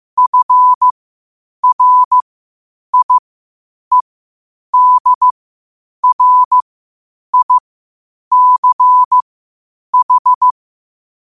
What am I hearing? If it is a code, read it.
Morse code, heard as a keyed tone, about 15 wpm, spaced as for 8 wpm, 1000 Hz. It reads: FRIEDRICH
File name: qotdfiles / 8wpm